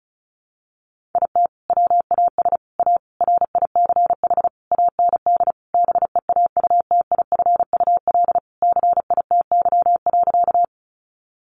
Morse code, heard as a keyed tone, about 34 wpm, 700 Hz. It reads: IT WAS A RICH AND BEAUTIFUL CITY.
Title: Sentences from Top 1100 Words 35wpm